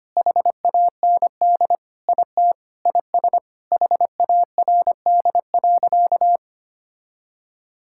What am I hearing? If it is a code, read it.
Morse code, heard as a keyed tone, about 25 wpm, 700 Hz. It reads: HAND IT IS HARD.
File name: practice / 25wpm / Sentences with 4-Word Phrases - 25wpm